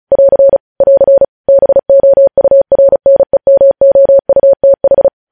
Morse code, heard as a keyed tone, about 36 wpm, 550 Hz. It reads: <AR> <AR> BOURNEMOUTH
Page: News headlines in Morse code at 35 wpm